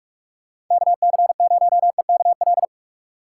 Morse code, heard as a keyed tone, about 42 wpm, 700 Hz. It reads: KC0EKL
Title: USA Call Signs - 6 Characters - 6x Character Spacing - 45wpm